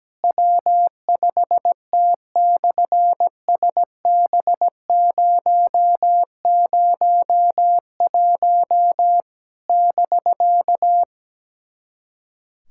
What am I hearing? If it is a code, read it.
Morse code, heard as a keyed tone, about 17 wpm, 700 Hz. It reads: W5T/SB001 <BK>
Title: SOTA QSO Practice 17wpm